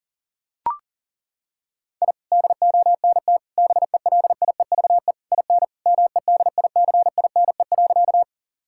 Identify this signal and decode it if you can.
Morse code, heard as a keyed tone, about 37 wpm, 700 Hz. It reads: I DONT BELIEVE IN MEDICINE.
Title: Sentences from Top 1000 Words - Review Part 6 - 40wpm